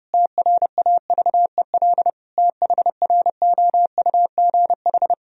Morse code, heard as a keyed tone, about 32 wpm, 700 Hz. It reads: TRAVEL THROUGH
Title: Sentences from Top 1000 Words - Review Part 2 - 30wpm